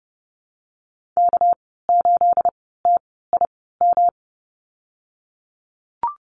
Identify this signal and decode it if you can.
Morse code, heard as a keyed tone, about 30 wpm, spaced as for 17 wpm, 700 Hz. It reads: K8TIM
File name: Lesson 37g - Letter x - Call Signs 30wpm